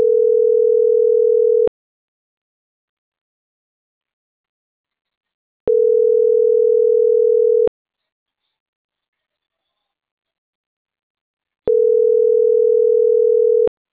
Ringback tone (North America), 440 + 480 Hz.